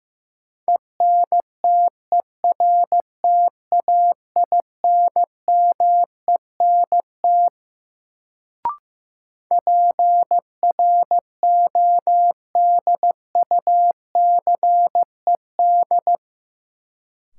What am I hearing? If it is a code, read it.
Morse code, heard as a keyed tone, about 15 wpm, 700 Hz. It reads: ENTERTAINMENT PRODUCED
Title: Sentences from Top 1000 Words - Review Part 6 - New Word Drill - 15wpm